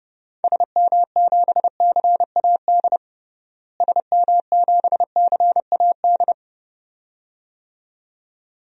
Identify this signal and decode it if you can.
Morse code, heard as a keyed tone, about 30 wpm, 700 Hz. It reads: SM7CAD SM7CAD
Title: DXCC-100 Call Signs - 6 Characters - Repeated 2x - 30wpm